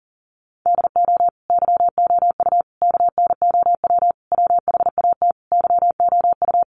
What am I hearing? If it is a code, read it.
Morse code, heard as a keyed tone, about 40 wpm, 700 Hz. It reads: DO YOU KNOW WHAT YOU
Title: Sentences from Top 100 Words 40wpm